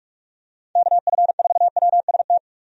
Morse code, heard as a keyed tone, about 46 wpm, 700 Hz. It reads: KF4WST
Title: USA Call Signs - 6 Characters - 45wpm